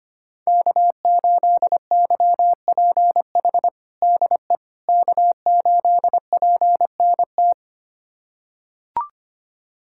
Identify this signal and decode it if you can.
Morse code, heard as a keyed tone, about 25 wpm, 700 Hz. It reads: K8YPH DE K8PNT
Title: USA Callsign DE Callsign - 5 Characters - 25wpm